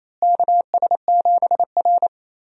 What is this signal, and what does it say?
Morse code, heard as a keyed tone, about 28 wpm, 700 Hz. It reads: KS7R
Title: USA Call Signs - 4 Characters - Repeated 3x - 28wpm